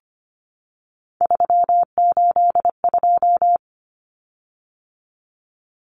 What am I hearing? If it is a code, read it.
Morse code, heard as a keyed tone, about 25 wpm, 700 Hz. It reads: 382
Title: Common Cut Numbers - 3-Digit 25wpm